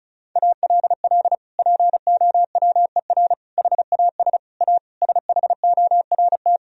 Morse code, heard as a keyed tone, about 34 wpm, 700 Hz. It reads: ALL POWER HAS A SHORT
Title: Sentences from Top 1000 Words - Review Part 2 - 35wpm